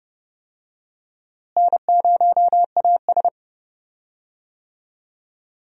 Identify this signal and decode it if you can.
Morse code, heard as a keyed tone, about 30 wpm, 700 Hz. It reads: N0AS
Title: USA Call Signs - 4 Characters - Repeated 2x - 30wpm